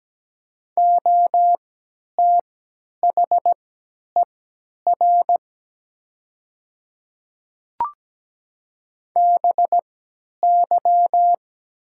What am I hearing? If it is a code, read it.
Morse code, heard as a keyed tone, about 17 wpm, spaced as for 10 wpm, 700 Hz. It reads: OTHER BY
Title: 3x Character Spacing - Top 100 Words - 17wpm